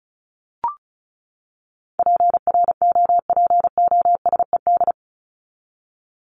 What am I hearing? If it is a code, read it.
Morse code, heard as a keyed tone, about 34 wpm, 700 Hz. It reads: PROPOSED
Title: ED Suffix - Top 100 Words with Base Words 35wpm